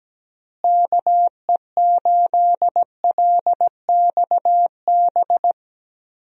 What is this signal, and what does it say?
Morse code, heard as a keyed tone, about 17 wpm, 700 Hz. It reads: KE8LXB
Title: USA Call Signs - 6 Characters - Repeated 2x - 17wpm